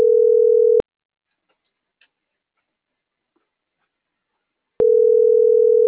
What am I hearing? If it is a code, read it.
Ringback tone (North America), 440 + 480 Hz.
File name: CENTRAL MAERSK / out luis.herrera - .28834